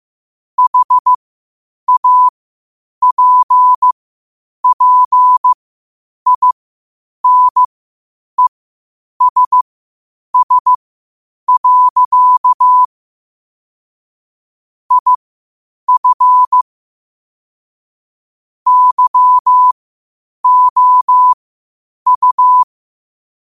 Morse code, heard as a keyed tone, about 15 wpm, spaced as for 8 wpm, 1000 Hz. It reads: HAPPINESS. IF YOU
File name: qotdfiles / 8wpm